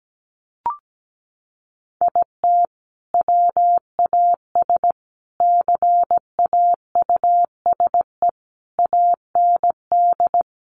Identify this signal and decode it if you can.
Morse code, heard as a keyed tone, about 17 wpm, 700 Hz. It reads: IT WAS CAUSE AND
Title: Sentences from Top 1000 Words 17wpm